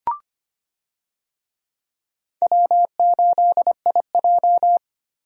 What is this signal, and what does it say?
Morse code, heard as a keyed tone, about 25 wpm, 700 Hz. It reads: W8IJ
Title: USA Call Signs - 4 Characters - Repeated 3x - 25wpm